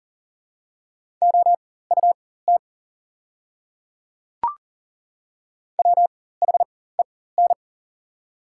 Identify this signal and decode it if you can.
Morse code, heard as a keyed tone, about 40 wpm, spaced as for 19 wpm, 690 Hz. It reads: OUT WHEN
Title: 4x Character Spacing - Top 100 Words - 40wpm